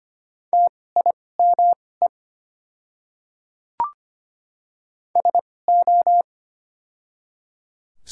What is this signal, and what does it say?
Morse code, heard as a keyed tone, about 25 wpm, spaced as for 18 wpm, 700 Hz. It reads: TIME SO
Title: 2x Character Spacing - Top 100 Words - 25wpm